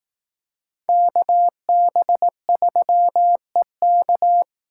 Morse code, heard as a keyed tone, about 19 wpm, 700 Hz. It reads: KB3EK
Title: Speed Racing - USA 2x2 Call Signs 15wpm